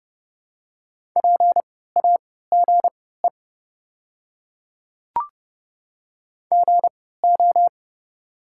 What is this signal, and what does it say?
Morse code, heard as a keyed tone, about 30 wpm, spaced as for 17 wpm, 700 Hz. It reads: PAGE GO
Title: 3x Character Spacing - Top 300 Words - 30wpm